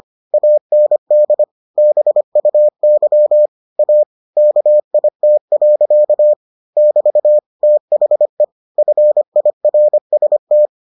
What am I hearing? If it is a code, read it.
Morse code, heard as a keyed tone, about 25 wpm, 600 Hz. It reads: AND BUY A KIT. <BT> THE FIRST